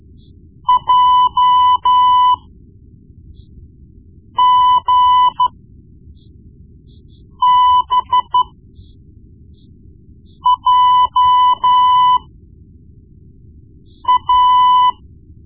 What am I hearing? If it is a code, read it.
Morse code, heard as a keyed tone, about 13 wpm, spaced as for 4 wpm, 1000 Hz. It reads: JGBJA
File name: Morse Lessons / Morse Lessons / Tape-1B